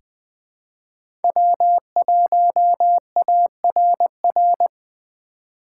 Morse code, heard as a keyed tone, about 20 wpm, 700 Hz. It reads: W1ARR
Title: 2023 POTA Hunter Call Signs - 20wpm